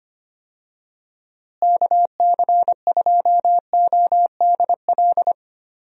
Morse code, heard as a keyed tone, about 25 wpm, 700 Hz. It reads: KC2ODL
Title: USA Call Signs - 6 Characters - Repeated 2x - 25wpm